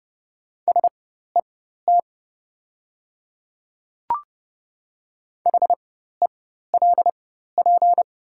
Morse code, heard as a keyed tone, about 30 wpm, spaced as for 14 wpm, 700 Hz. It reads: SET HELP
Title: 4x Character Spacing - Top 200 Words - 30wpm